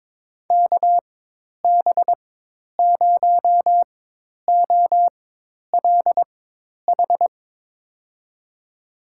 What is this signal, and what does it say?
Morse code, heard as a keyed tone, about 22 wpm, spaced as for 10 wpm, 700 Hz. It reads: KB0OLH
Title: USA Call Signs - 4x Character Spacing - 22wpm